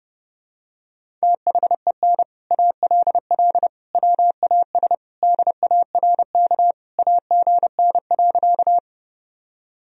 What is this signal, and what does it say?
Morse code, heard as a keyed tone, about 32 wpm, 700 Hz. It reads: THEN ALL WAS DARK AGN.
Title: Sentences from Top 500 Words and Abbreviations 30wpm